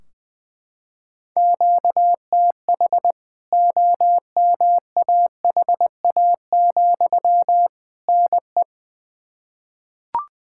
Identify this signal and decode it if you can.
Morse code, heard as a keyed tone, about 20 wpm, 700 Hz. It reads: QTH OMAHA, NE
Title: K1USN SST QSO Practice 20wpm